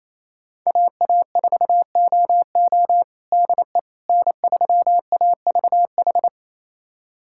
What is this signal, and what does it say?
Morse code, heard as a keyed tone, about 28 wpm, 700 Hz. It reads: AA4OO DE N3AVH
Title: USA Callsign DE Callsign - 5 Characters - 28wpm